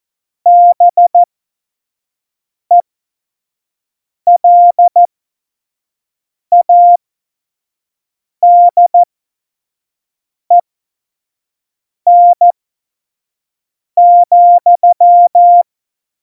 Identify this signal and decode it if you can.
Morse code, heard as a keyed tone, about 14 wpm, spaced as for 5 wpm, 700 Hz. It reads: BELADEN,
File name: Morsekurs / Witze / Witz 19. Februar 2019, 25 F70